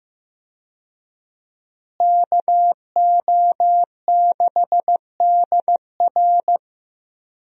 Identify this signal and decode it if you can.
Morse code, heard as a keyed tone, about 15 wpm, 700 Hz. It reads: KO6DR
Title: USA Call Signs - 5 Characters - Repeated 2x - 15wpm